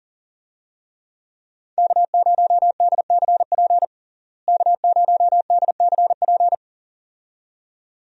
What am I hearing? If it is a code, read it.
Morse code, heard as a keyed tone, about 37 wpm, 700 Hz. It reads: K0DCP K0DCP
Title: USA Call Signs - 5 Characters - Repeated 2x - 40wpm